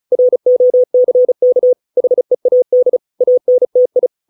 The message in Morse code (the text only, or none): ROCK HEAD ANTI